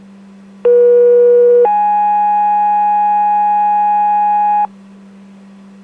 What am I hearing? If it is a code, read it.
Two-tone page, heard as a one-second tone, then a three-second tone, about 480 then 830 Hz, with noise. It is heard from 0.6 to 4.7 s.